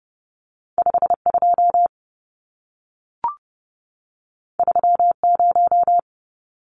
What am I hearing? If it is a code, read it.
Morse code, heard as a keyed tone, about 30 wpm, 700 Hz. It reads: 52 30
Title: Numbers - 2-Digit 30wpm